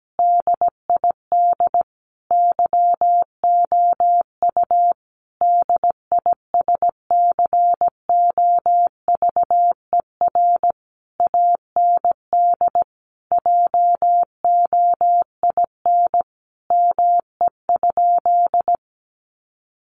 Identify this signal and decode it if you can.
Morse code, heard as a keyed tone, about 17 wpm, 700 Hz. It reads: DID YOU DISCOVER AND JOIN ME?
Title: Sentences from Top 1400 Words 17wpm